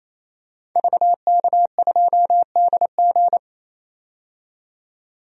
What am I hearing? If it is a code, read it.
Morse code, heard as a keyed tone, about 28 wpm, 700 Hz. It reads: VK2DG
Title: DXCC-100 Call Signs - 5 Characters - 28wpm